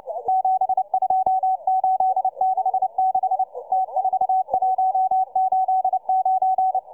Morse code, heard as a keyed tone, about 29 wpm, 760 Hz, with radio noise. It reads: 72866 T4189